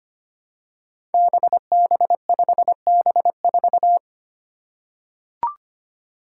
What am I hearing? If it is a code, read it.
Morse code, heard as a keyed tone, about 25 wpm, 700 Hz. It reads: BB5B4
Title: Five Challenging Letter-Number Combinations 25wpm